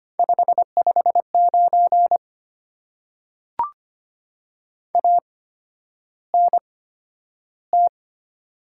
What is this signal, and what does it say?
Morse code, heard as a keyed tone, about 25 wpm, spaced as for 11 wpm, 690 Hz. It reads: 559 ANT
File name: Top 90 QSO Elements - 8x Character Spacing - 25wpm